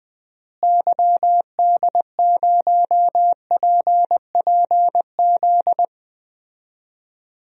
Morse code, heard as a keyed tone, about 20 wpm, 700 Hz. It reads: YD0PPZ
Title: DXCC-100 Call Signs - 6 Characters - Repeated 3x - 20wpm